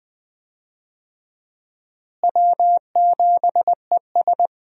Morse code, heard as a keyed tone, about 20 wpm, 700 Hz. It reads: W7ES
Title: USA Call Signs - 4 Characters - Repeated 2x - 20wpm